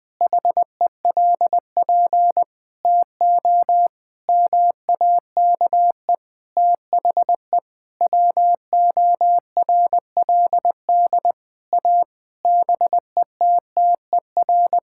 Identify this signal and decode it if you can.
Morse code, heard as a keyed tone, about 20 wpm, 700 Hz. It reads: HELP TO MAKE THE WORLD A BETTER
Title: Sentences with 6-Word Phrases - 20wpm